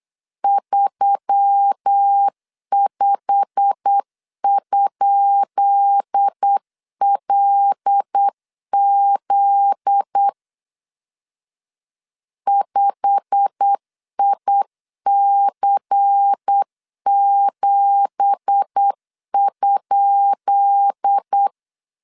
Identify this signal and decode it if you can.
Morse code, heard as a keyed tone, about 8 wpm, 800 Hz. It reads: 35?LZ 5IC7?
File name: LEZIONI / 10 SIGNS / 10B INT 8wpm 3534